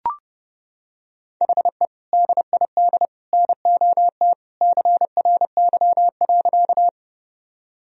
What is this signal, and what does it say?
Morse code, heard as a keyed tone, about 30 wpm, 700 Hz. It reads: HE DID NOT CRY.
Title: Sentences from Top 1000 Words - Review Part 2 - 30wpm